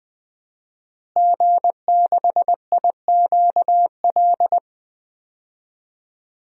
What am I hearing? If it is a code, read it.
Morse code, heard as a keyed tone, about 20 wpm, 700 Hz. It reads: G6IQL
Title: DXCC-100 Call Signs - 5 Characters - Repeated 2x - 20wpm